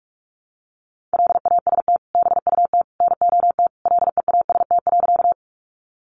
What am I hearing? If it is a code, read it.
Morse code, heard as a keyed tone, about 44 wpm, 700 Hz. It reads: LAST BUT NOT LEAST.
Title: Sentences from Top 700 Words 45wpm